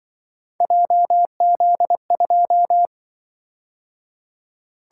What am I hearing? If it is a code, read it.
Morse code, heard as a keyed tone, about 24 wpm, 700 Hz. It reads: JZ2
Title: Speed-Racing - Three Letter-Number Combinations - 20wpm